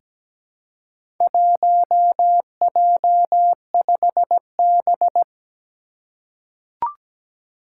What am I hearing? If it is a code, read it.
Morse code, heard as a keyed tone, about 17 wpm, 700 Hz. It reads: 1J5B
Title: Four Challenging Letter-Number Combinations 17wpm